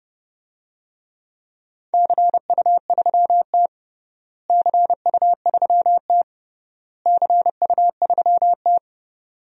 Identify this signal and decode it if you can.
Morse code, heard as a keyed tone, about 30 wpm, 700 Hz. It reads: CU3T CU3T CU3T
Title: DXCC-100 Call Signs - 4 Characters - Repeated 3x - 30wpm